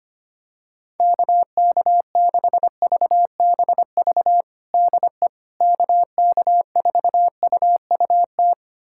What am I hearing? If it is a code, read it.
Morse code, heard as a keyed tone, about 25 wpm, 700 Hz. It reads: KK6VBV DE KK4UUT
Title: USA Callsign DE Callsign - 6 Characters - 25wpm